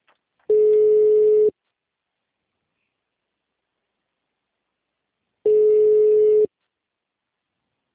Ringback tone (Europe, 425 Hz).
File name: out- -2004- - - .22267